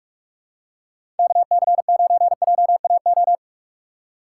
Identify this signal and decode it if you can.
Morse code, heard as a keyed tone, about 44 wpm, 700 Hz. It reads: KC9JAO
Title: USA Call Signs - 6 Characters - 3x Character Spacing - 45wpm